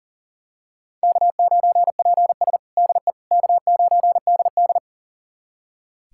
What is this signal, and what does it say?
Morse code, heard as a keyed tone, about 40 wpm, 700 Hz. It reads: K9PS DE K9DD
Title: USA Callsign DE Callsign - 4 Characters - 40wpm